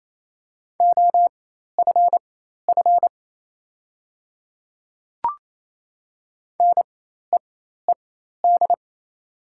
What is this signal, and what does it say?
Morse code, heard as a keyed tone, about 28 wpm, spaced as for 13 wpm, 700 Hz. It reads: OFF NEED
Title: 4x Character Spacing - Top 500 Words - 28wpm